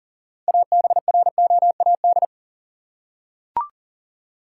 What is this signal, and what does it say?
Morse code, heard as a keyed tone, about 39 wpm, 700 Hz. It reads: ABROAD
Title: Six Letter Words - 40wpm